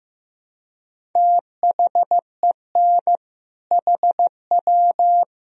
Morse code, heard as a keyed tone, about 15 wpm, 700 Hz. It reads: THEN HW
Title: Sentences from Top 100 Words and Abbreviations 15wpm